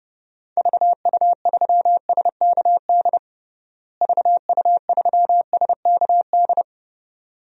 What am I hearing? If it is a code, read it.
Morse code, heard as a keyed tone, about 30 wpm, 700 Hz. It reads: VU3SKD VU3SKD
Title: DXCC-100 Call Signs - 6 Characters - Repeated 2x - 30wpm